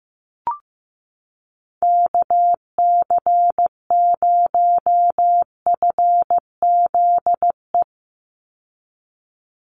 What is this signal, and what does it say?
Morse code, heard as a keyed tone, about 15 wpm, 700 Hz. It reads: KC0FZE
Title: USA Call Signs 15wpm